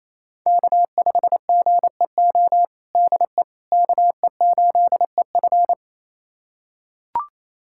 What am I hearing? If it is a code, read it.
Morse code, heard as a keyed tone, about 28 wpm, 700 Hz. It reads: K5GEO DE KE8EF
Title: USA Callsign DE Callsign - 5 Characters - 28wpm